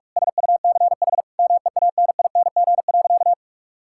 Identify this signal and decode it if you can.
Morse code, heard as a keyed tone, about 41 wpm, 700 Hz. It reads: SUCH MEANING.
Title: Sentences from Top 1000 Words - Review Part 6 - 45wpm